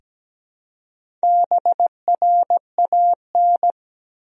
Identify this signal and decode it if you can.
Morse code, heard as a keyed tone, about 17 wpm, 700 Hz. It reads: BRAN
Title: USA Common Names - Max 5 Letters - 17wpm